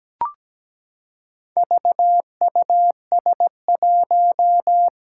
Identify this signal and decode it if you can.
Morse code, heard as a keyed tone, about 17 wpm, 700 Hz. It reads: VUS1
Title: Four Challenging Letter-Number Combinations 17wpm